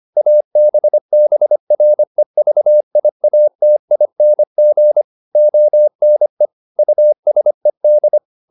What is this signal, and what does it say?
Morse code, heard as a keyed tone, about 25 wpm, 600 Hz. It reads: ABBREVIATING ONE USED